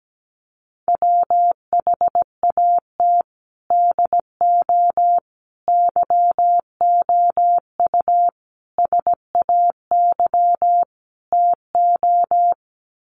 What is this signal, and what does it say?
Morse code, heard as a keyed tone, about 17 wpm, 700 Hz. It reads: WHAT DO YOU SAY TO
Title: Sets of 5 Words 17wpm